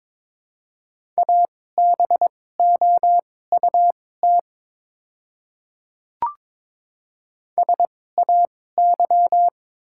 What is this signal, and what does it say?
Morse code, heard as a keyed tone, about 22 wpm, spaced as for 16 wpm, 700 Hz. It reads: ABOUT SAY
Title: 2x Character Spacing - Top 300 Words - 22wpm